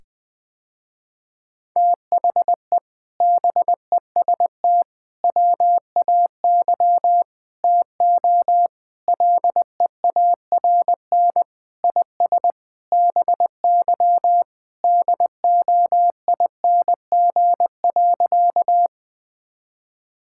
Morse code, heard as a keyed tone, about 21 wpm, 700 Hz. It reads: THE BEST WAY TO LEARN IS BY DOING.
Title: Sentences with 4-Word Phrases - Spoken Beginning - 20wpm